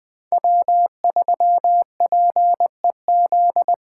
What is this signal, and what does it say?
Morse code, heard as a keyed tone, about 20 wpm, 700 Hz. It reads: W3PEZ
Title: USA Call Signs - 5 Characters - 2x Character Spacing - 20wpm